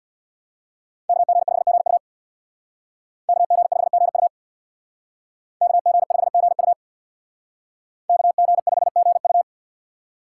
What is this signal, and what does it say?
Morse code, heard as a keyed tone, about 63 wpm, 700 Hz. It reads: KG5GU KG5GU KG5GU KG5GU